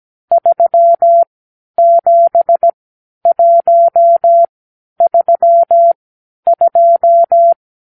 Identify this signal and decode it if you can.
Morse code, heard as a keyed tone, about 17 wpm, spaced as for 10 wpm, 680 Hz. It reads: 37132
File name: k7qo / Lesson 52 (8)